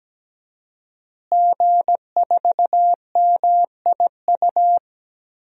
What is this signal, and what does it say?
Morse code, heard as a keyed tone, about 17 wpm, 700 Hz. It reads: G4MIU
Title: DXCC-100 Call Signs - 5 Characters - Repeated 2x - 17wpm